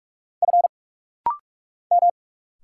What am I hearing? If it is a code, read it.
Morse code, heard as a keyed tone, about 46 wpm, 690 Hz. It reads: F M
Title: Single Letters - Rapid-Fire 45wpm